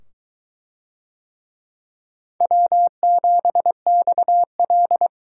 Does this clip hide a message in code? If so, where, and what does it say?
Yes: Morse code from 2.4 to 5.1 s, heard as a keyed tone, about 21 wpm, 700 Hz. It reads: W7XL